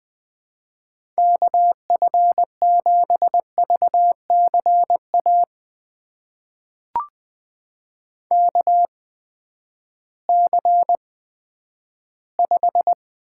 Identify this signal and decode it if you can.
Morse code, heard as a keyed tone, about 20 wpm, spaced as for 11 wpm, 700 Hz. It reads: KF7VCA KC5
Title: USA Call Signs - 8x Character Spacing - 20wpm